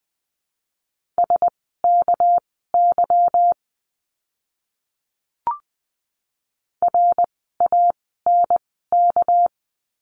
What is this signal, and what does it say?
Morse code, heard as a keyed tone, about 20 wpm, spaced as for 14 wpm, 700 Hz. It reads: SKY RANK
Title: 199 Lesson 32f - Letter k - Words 20wpm